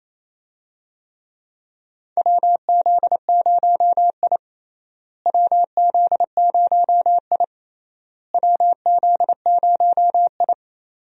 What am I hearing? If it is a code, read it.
Morse code, heard as a keyed tone, about 28 wpm, 700 Hz. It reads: WZ0I WZ0I WZ0I
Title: USA Call Signs - 4 Characters - Repeated 3x - 28wpm